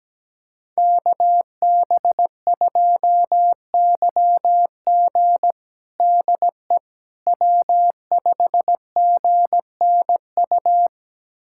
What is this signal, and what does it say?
Morse code, heard as a keyed tone, about 17 wpm, 700 Hz. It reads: KB2YG DE W5GNU
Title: USA Callsign DE Callsign - 5 Characters - 17wpm